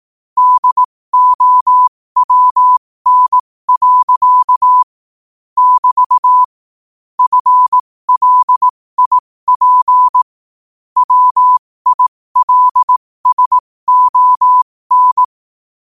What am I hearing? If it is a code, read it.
Morse code, heard as a keyed tone, about 18 wpm, spaced as for 15 wpm, 1000 Hz. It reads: DOWN. <BT> FLIP WILSON
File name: qotdfiles / 15wpm